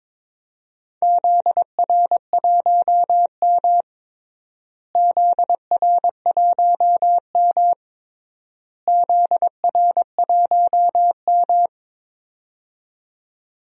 Morse code, heard as a keyed tone, about 22 wpm, 700 Hz. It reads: ZR1M ZR1M ZR1M
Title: DXCC-100 Call Signs - 4 Characters - Repeated 3x - 22wpm